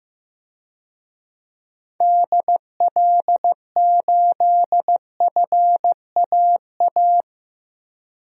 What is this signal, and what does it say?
Morse code, heard as a keyed tone, about 15 wpm, 700 Hz. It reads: DL8FAA
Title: DXCC-100 Call Signs - 6 Characters - Repeated 2x - 15wpm